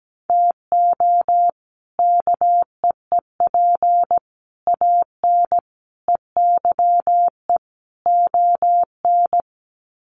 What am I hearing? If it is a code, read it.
Morse code, heard as a keyed tone, about 17 wpm, 700 Hz. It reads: TO KEEP AN EYE ON
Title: Sets of 5 Words Encore 17wpm